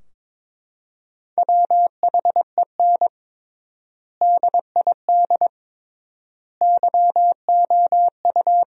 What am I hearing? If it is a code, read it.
Morse code, heard as a keyed tone, about 22 wpm, 700 Hz. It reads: WHEN DID YOU
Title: 3x Word-Spacing - Sentences from Top 300 Words - 22wpm